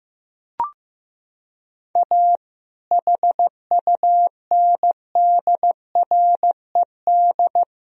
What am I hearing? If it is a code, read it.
Morse code, heard as a keyed tone, about 15 wpm, 700 Hz. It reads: A HUNDRED
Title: Sentences from Top 1000 Words - Review Part 2 - 15wpm